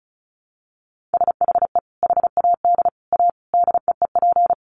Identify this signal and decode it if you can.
Morse code, heard as a keyed tone, about 34 wpm, 700 Hz. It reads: SHE HAD A DEEP
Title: Sentences from Top 1300 Words 35wpm